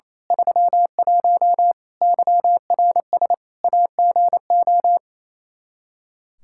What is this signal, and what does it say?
Morse code, heard as a keyed tone, about 27 wpm, 700 Hz. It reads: 31 YRS AGO
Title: Practice QSO - Exchange 3 - More Info - 28wpm